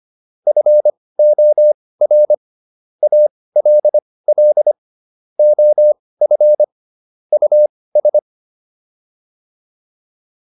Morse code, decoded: FOR ALL OF US

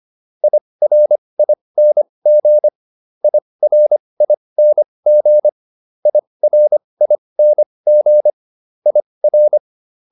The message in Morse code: IRING IRING IRING IR

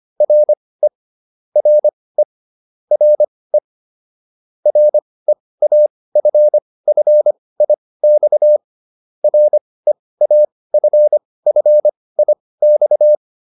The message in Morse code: RE RE RE REAFFIX REAFFIX

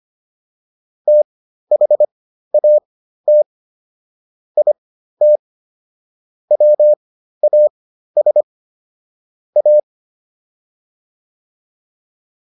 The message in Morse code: THAT IT WAS A

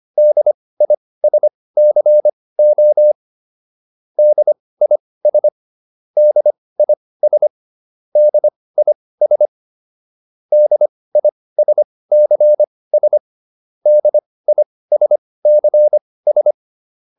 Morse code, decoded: DISCO DIS DIS DIS DISCS DISCS

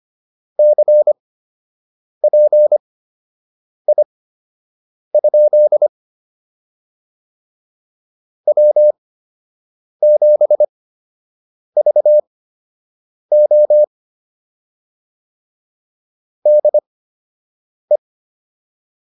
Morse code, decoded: CPI? W7VO DE

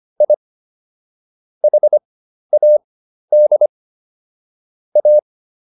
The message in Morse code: I HAD A

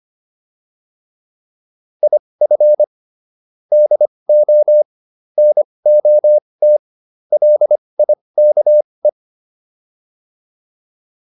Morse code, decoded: IF DO NOT LIKE

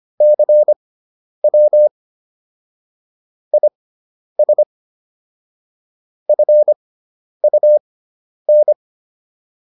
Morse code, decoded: CW IS FUN

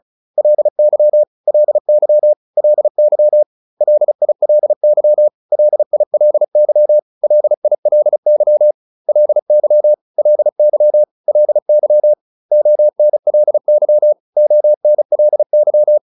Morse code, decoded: LY LY LY LILY LILY LILY LY LY LY ONLY ONLY

